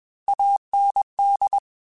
Morse code, heard as a keyed tone, about 21 wpm, 800 Hz. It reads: AND
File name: Technical Subjects / Technical Subjects-mp3s / AND Gate @20